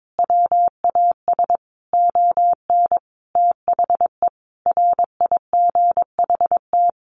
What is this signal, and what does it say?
Morse code, heard as a keyed tone, about 22 wpm, 700 Hz. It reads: WAS ON THE RIGHT
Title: Sentences from Top 1100 Words 22wpm